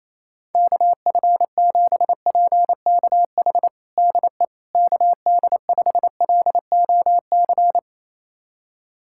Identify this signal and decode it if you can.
Morse code, heard as a keyed tone, about 28 wpm, 700 Hz. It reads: KF7PKH DE KD5LOC
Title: USA Callsign DE Callsign 28wpm